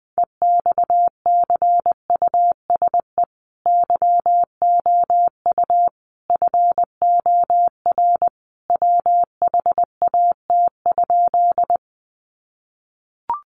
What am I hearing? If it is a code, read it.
Morse code, heard as a keyed tone, about 20 wpm, 700 Hz. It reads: EXCUSE YOU FOR WHAT?
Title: Sentences from Top 700 Words 20wpm